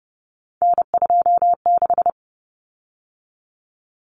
Morse code, heard as a keyed tone, about 30 wpm, 700 Hz. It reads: N26
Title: Common Cut Numbers - 3-Digit 30wpm